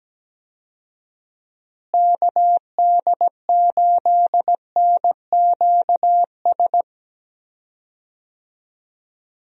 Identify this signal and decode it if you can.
Morse code, heard as a keyed tone, about 17 wpm, 700 Hz. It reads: KD8NQS
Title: USA Call Signs - 6 Characters - Repeated 3x - 17wpm